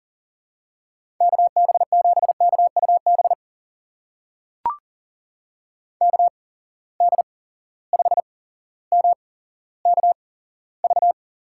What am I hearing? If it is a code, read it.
Morse code, heard as a keyed tone, about 40 wpm, spaced as for 17 wpm, 700 Hz. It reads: KB7KUB KD5MKV
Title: USA Call Signs - 6 Characters - 8x Character Spacing - 40wpm